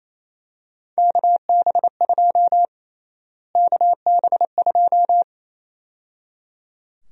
Morse code, heard as a keyed tone, about 28 wpm, 700 Hz. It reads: KB2 KB2
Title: USA Call Sign Prefixes - Repeated 2x - 28wpm